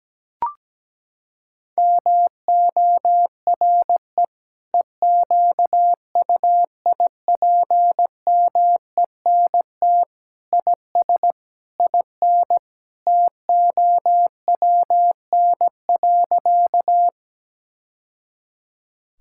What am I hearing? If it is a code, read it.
Morse code, heard as a keyed tone, about 17 wpm, 700 Hz. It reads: MORE EQUIPMENT IS IN TOWN.
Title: Sentences from Top 1000 Words - Review Part 5 - 17wpm